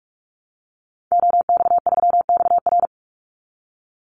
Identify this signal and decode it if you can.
Morse code, heard as a keyed tone, about 44 wpm, 700 Hz. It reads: OX3XR
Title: Call Signs Repeated 3x - 45wpm